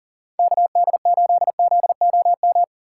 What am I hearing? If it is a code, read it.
Morse code, heard as a keyed tone, about 40 wpm, 700 Hz. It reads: KD8ZOM
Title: USA Call Signs - 8x Character Spacing - 40wpm